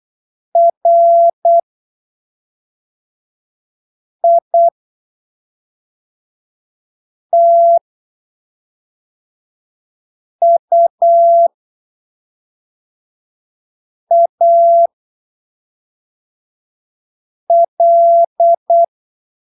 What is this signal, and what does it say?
Morse code, heard as a keyed tone, about 8 wpm, spaced as for 3 wpm, 680 Hz. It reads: RITUAL
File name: arquivos / texto / texto 08 08ppm